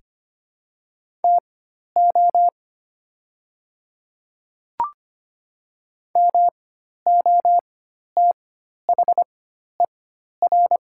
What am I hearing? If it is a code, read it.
Morse code, heard as a keyed tone, about 25 wpm, spaced as for 12 wpm, 690 Hz. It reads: TO MOTHER
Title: 4x Character Spacing - Top 300 Words - 25wpm